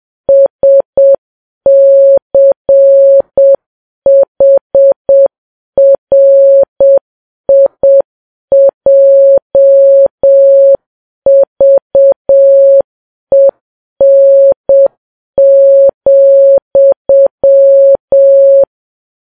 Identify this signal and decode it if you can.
Morse code, heard as a keyed tone, about 7 wpm, 550 Hz. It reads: SCHRIJVEN,
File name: Files Web tls / Morse on9fv / morse7kps